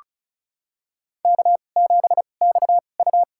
Morse code, heard as a keyed tone, about 34 wpm, spaced as for 25 wpm, 700 Hz. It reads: K7XU
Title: USA Call Signs - 2x Character Spacing - 35wpm